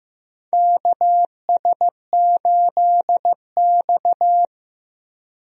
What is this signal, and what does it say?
Morse code, heard as a keyed tone, about 15 wpm, 700 Hz. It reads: KS8X
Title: USA Call Signs - 4 Characters - Repeated 3x - 15wpm